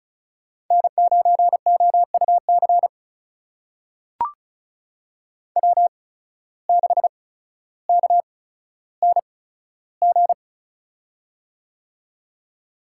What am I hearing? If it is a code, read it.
Morse code, heard as a keyed tone, about 36 wpm, spaced as for 15 wpm, 700 Hz. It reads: N9OUC W6KNG
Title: USA Call Signs - 5 Characters - 8x Character Spacing - 35wpm